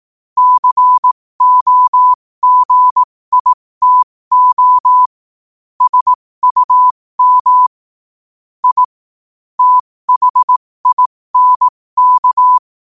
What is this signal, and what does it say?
Morse code, heard as a keyed tone, about 18 wpm, spaced as for 15 wpm, 1000 Hz. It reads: COGITO SUM I THINK
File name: qotdfiles / 15wpm